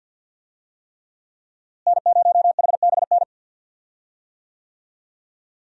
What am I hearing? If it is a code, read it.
Morse code, heard as a keyed tone, about 46 wpm, 700 Hz. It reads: N0HBN
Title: USA Call Signs - 5 Characters - Repeated 3x - 50wpm